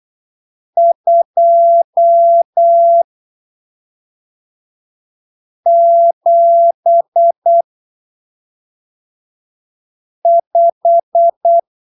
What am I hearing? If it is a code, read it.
Morse code, heard as a keyed tone, about 8 wpm, spaced as for 3 wpm, 680 Hz. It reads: 275
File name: arquivos / texto / texto 12 08ppm